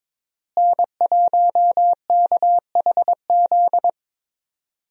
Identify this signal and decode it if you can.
Morse code, heard as a keyed tone, about 22 wpm, 700 Hz. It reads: N1KHZ
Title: USA Call Signs - 5 Characters - Repeated 2x - 22wpm